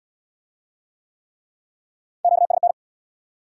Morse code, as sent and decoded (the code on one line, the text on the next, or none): --.- ... -.
QSN